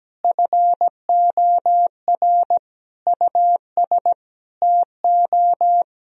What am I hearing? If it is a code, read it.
Morse code, heard as a keyed tone, about 17 wpm, 700 Hz. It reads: FOR US TO
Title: Sentences from Top 1000 Words - Review Part 4 - 17wpm